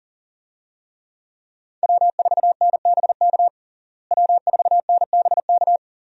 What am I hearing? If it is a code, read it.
Morse code, heard as a keyed tone, about 40 wpm, 700 Hz. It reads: W4NBK W4NBK
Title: USA Call Signs - 5 Characters - Repeated 2x - 40wpm